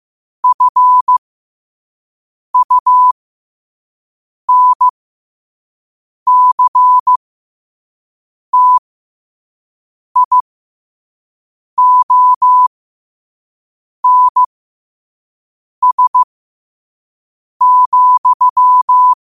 Morse code, heard as a keyed tone, about 15 wpm, spaced as for 5 wpm, 1000 Hz. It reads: FUNCTIONS,
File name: qotdfiles / 5wpm